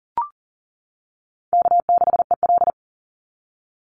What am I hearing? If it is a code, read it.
Morse code, heard as a keyed tone, about 40 wpm, 700 Hz. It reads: K6EL
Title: NA SOTA Chaser Callsigns 40wpm